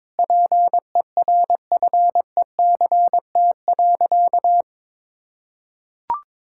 Morse code, heard as a keyed tone, about 22 wpm, 700 Hz. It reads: PERFECT.
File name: practice / 22wpm / Sentences with 3-Word Phrases - 22wpm